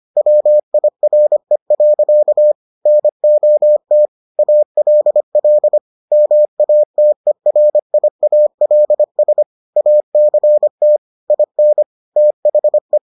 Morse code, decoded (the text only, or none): WIRE. NOT ALL MATERIALS ACT IN THE